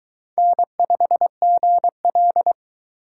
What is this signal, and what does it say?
Morse code, heard as a keyed tone, about 22 wpm, 700 Hz. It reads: N5GL